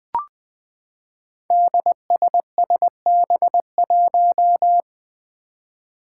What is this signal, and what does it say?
Morse code, heard as a keyed tone, about 20 wpm, 700 Hz. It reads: DSSB1
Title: Five Challenging Letter-Number Combinations 20wpm